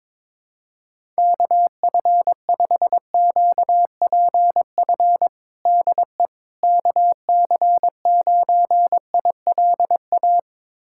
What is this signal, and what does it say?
Morse code, heard as a keyed tone, about 22 wpm, 700 Hz. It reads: KF5QPF DE KC9ILA
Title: USA Callsign DE Callsign 22wpm